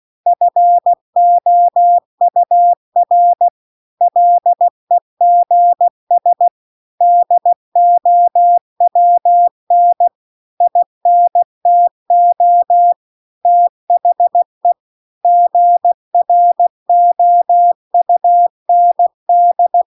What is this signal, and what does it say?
Morse code, heard as a keyed tone, about 16 wpm, 700 Hz. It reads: FOUR LEGS DOWN INTO THE GROUND